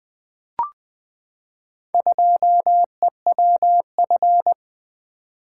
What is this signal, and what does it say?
Morse code, heard as a keyed tone, about 20 wpm, 700 Hz. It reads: 2EWF
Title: Four Letter-Number Combinations - 20wpm